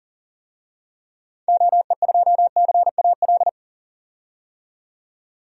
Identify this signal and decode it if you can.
Morse code, heard as a keyed tone, about 39 wpm, 700 Hz. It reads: OE2CAL
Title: DXCC-100 Call Signs - 6 Characters - 40wpm